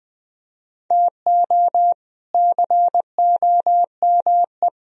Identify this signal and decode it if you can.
Morse code, heard as a keyed tone, about 20 wpm, 700 Hz. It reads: TO COME